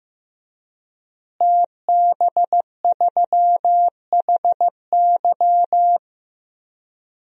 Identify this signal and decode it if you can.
Morse code, heard as a keyed tone, about 15 wpm, 700 Hz. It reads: TB3HY
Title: DXCC-100 Call Signs - 5 Characters - Repeated 3x - 15wpm